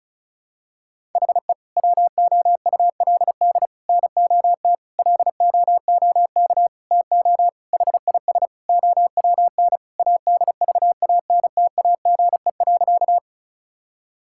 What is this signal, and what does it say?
Morse code, heard as a keyed tone, about 34 wpm, 700 Hz. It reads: HE WOULD NOT LOOK TO HIS OWN ADVANTAGE.
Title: Sentences from Top 1000 Words - Review Part 6 - 35wpm